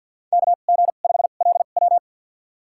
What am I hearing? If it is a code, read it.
Morse code, heard as a keyed tone, about 47 wpm, spaced as for 35 wpm, 700 Hz. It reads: KG5LW
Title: USA Call Signs - 5 Characters - 2x Character Spacing - 50wpm